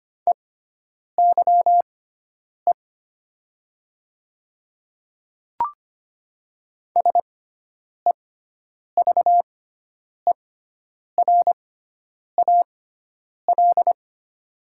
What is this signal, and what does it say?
Morse code, heard as a keyed tone, about 25 wpm, spaced as for 9 wpm, 700 Hz. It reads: EYE SEVERAL
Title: 6x Character Spacing - Top 500 Words - 25wpm